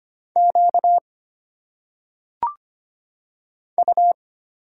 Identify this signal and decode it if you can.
Morse code, heard as a keyed tone, about 25 wpm, 700 Hz. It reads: Q U
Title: Single Letters - Phonetic Pronunciation 25wpm